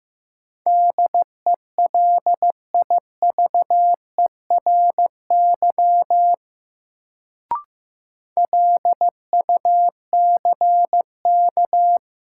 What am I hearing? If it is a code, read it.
Morse code, heard as a keyed tone, about 15 wpm, 700 Hz. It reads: DELIVERY LUCK
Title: Sentences from Top 1500 Words - New Word Drill - 15wpm